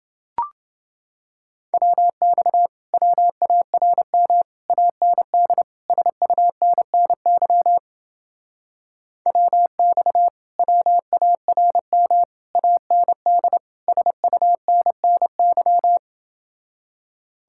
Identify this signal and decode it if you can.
Morse code, heard as a keyed tone, about 30 wpm, 700 Hz. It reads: WX WARM AND SUNNY WX WARM AND SUNNY